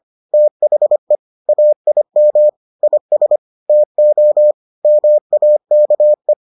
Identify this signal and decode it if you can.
Morse code, heard as a keyed tone, about 25 wpm, 600 Hz. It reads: THE AIM IS TO MAKE